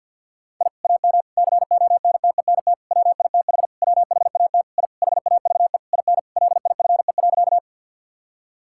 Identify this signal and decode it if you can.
Morse code, heard as a keyed tone, about 52 wpm, 700 Hz. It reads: I AM CONTENT WITH WHAT I HAVE IN LIFE.
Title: Sentences from Top 1000 Words - Review Part 6 - 50wpm